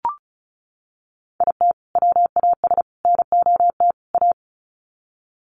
Morse code, heard as a keyed tone, about 34 wpm, 700 Hz. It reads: IT WAS NOT A
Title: Sets of 4 Words 35wpm